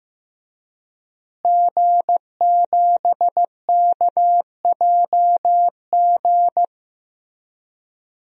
Morse code, heard as a keyed tone, about 15 wpm, 700 Hz. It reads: G7KJG
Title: DXCC-100 Call Signs - 5 Characters - Repeated 3x - 15wpm